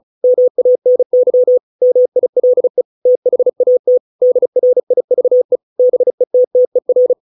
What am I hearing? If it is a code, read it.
Morse code, heard as a keyed tone, about 34 wpm, 500 Hz. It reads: MANY MILE THAT DRIVE BETTER